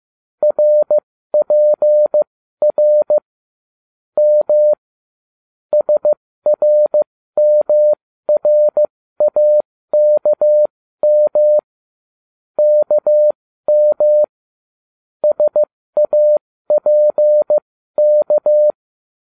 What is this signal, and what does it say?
Morse code, heard as a keyed tone, about 15 wpm, spaced as for 13 wpm, 600 Hz. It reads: RPR M SRMRAKM KM SAPK